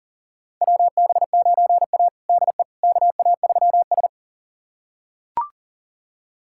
Morse code, heard as a keyed tone, about 40 wpm, 700 Hz. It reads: WB9A DE KA3S
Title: USA Callsign DE Callsign - 4 Characters - 40wpm